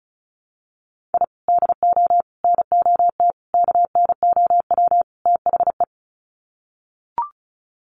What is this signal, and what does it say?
Morse code, heard as a keyed tone, about 34 wpm, 700 Hz. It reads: I DO NOT KNOW THE
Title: Sets of 5 Words 35wpm